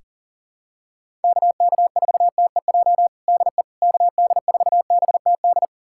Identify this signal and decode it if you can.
Morse code, heard as a keyed tone, about 40 wpm, 700 Hz. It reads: KK4TEJ DE KD4BTD
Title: USA Callsign DE Callsign - 6 Characters - 40wpm